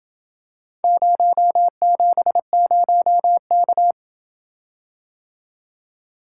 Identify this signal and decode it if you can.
Morse code, heard as a keyed tone, about 28 wpm, 700 Hz. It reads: 070K